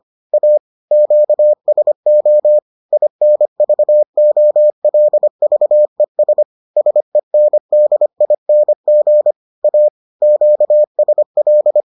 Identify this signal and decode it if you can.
Morse code, heard as a keyed tone, about 25 wpm, 600 Hz. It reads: A QSO INVOLVES SENDING A QSL